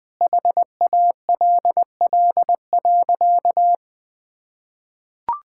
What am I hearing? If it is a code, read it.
Morse code, heard as a keyed tone, about 21 wpm, 700 Hz. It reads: HALL.
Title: Sentences with 8-Word Phrases - Spoken Beginning - 20wpm